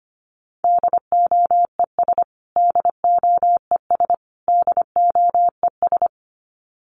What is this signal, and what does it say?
Morse code, heard as a keyed tone, about 25 wpm, 700 Hz. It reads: DOES DOES DOES
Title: Top 200 Words Repeated 25wpm